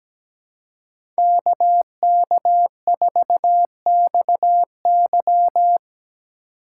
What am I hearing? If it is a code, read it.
Morse code, heard as a keyed tone, about 17 wpm, 700 Hz. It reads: KK4XY
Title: USA Call Signs - 5 Characters - 2x Character Spacing - 17wpm